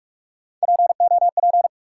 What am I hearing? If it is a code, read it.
Morse code, heard as a keyed tone, about 44 wpm, 700 Hz. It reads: POP
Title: Three Letter Words - 45wpm